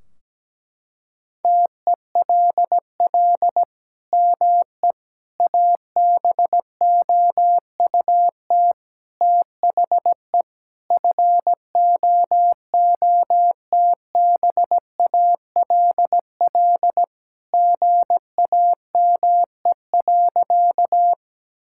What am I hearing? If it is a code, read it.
Morse code, heard as a keyed tone, about 17 wpm, 700 Hz. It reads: TELL ME ABOUT THE FOOTBALL GAME.
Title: Sentences from Top 1600 Words 17wpm